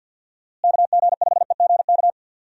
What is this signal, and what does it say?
Morse code, heard as a keyed tone, about 52 wpm, 700 Hz. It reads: KG5EGK
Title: USA Call Signs - 6 Characters - Repeated 2x - 50wpm